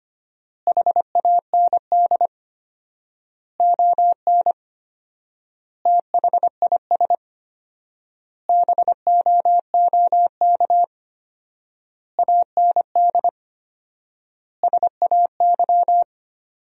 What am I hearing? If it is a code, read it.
Morse code, heard as a keyed tone, about 25 wpm, 700 Hz. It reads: HAND ON THIS BOOK AND SAY
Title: 4x Word-Spacing - Sentences from Top 400 Words - 25wpm